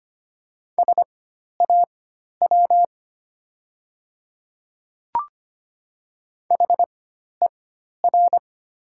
Morse code, heard as a keyed tone, about 25 wpm, spaced as for 12 wpm, 700 Hz. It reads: SAW HER
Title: 4x Character Spacing - Top 500 Words - 25wpm